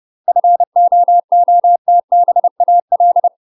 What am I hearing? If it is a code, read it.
Morse code, heard as a keyed tone, about 30 wpm, 700 Hz. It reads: FOOTBAL
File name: UK MASTER QSOs TRAINING / 30WPM / UK MASTER QSOs 0040